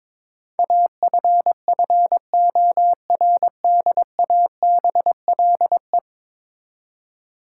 Morse code, heard as a keyed tone, about 21 wpm, 700 Hz. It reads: AFFORDABLE
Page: morse-code-ninja-practice-22wpm